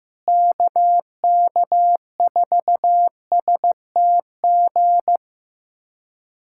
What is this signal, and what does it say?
Morse code, heard as a keyed tone, about 15 wpm, 700 Hz. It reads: KK4STG
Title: USA Callsign DE Callsign 15wpm